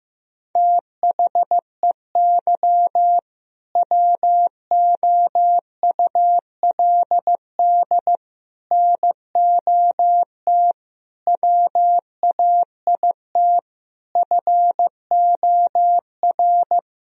Morse code, heard as a keyed tone, about 15 wpm, 700 Hz. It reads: THEY WOULD NOT WAIT FOR
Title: Sentences from Top 1000 Words - Review Part 3 - 15wpm